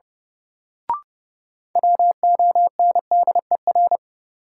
Morse code, heard as a keyed tone, about 30 wpm, 700 Hz. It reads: WONDER
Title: Sentences from Top 500 Words - New Word Drill - 30wpm